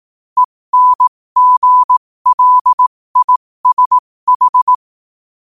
Morse code, heard as a keyed tone, about 18 wpm, spaced as for 15 wpm, 1000 Hz. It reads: ENGLISH
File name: qotdfiles / 15wpm